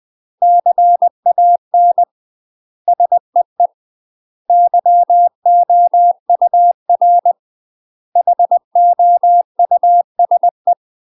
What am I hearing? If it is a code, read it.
Morse code, heard as a keyed tone, about 20 wpm, 700 Hz. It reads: CAN SEE YOUR HOUSE
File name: UK MASTER QSOs TRAINING / 20WPM / UK MASTER QSOs 0066